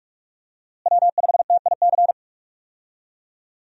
Morse code, heard as a keyed tone, about 42 wpm, 700 Hz. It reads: W5TIC
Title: USA Call Signs - 5 Characters - Repeated 2x - 45wpm